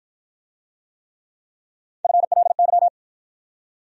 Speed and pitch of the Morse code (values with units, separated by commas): 50 wpm, 700 Hz